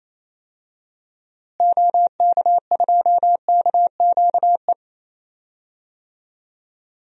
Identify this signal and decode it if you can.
Morse code, heard as a keyed tone, about 28 wpm, 700 Hz. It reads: OK2KQE
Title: DXCC-100 Call Signs - 6 Characters - Repeated 2x - 28wpm